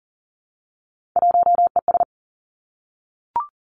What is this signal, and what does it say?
Morse code, heard as a keyed tone, about 41 wpm, 690 Hz. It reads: 1ES
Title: Three Challenging Letter-Number Combinations 40wpm